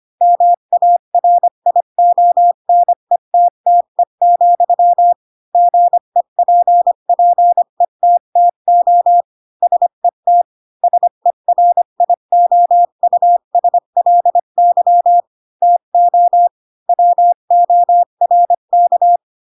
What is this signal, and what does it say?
Morse code, heard as a keyed tone, about 25 wpm, 700 Hz. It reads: MARIONETTE, GEPPETTO SET SERIOUSLY TO WORK
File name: Pinocchio / Pinocchio ch3 20 25 ik1hge